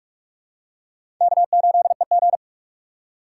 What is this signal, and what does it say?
Morse code, heard as a keyed tone, about 47 wpm, 700 Hz. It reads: K8EG
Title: USA Call Signs - 4 Characters - 3x Character Spacing - 45wpm